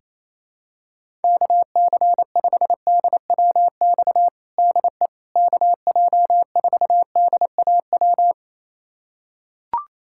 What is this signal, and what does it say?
Morse code, heard as a keyed tone, about 28 wpm, 700 Hz. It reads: KC5DWX DE KJ4DAW
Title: USA Callsign DE Callsign - 6 Characters - 28wpm